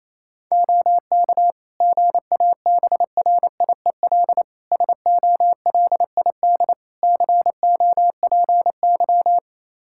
Morse code, heard as a keyed tone, about 27 wpm, 700 Hz. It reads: OK GABRIEL SOLID COPY
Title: Practice QSO - Exchange 3 - More Info - 28wpm